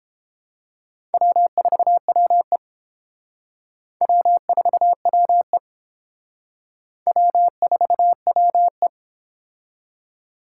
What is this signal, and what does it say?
Morse code, heard as a keyed tone, about 30 wpm, 700 Hz. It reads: W4WE W4WE W4WE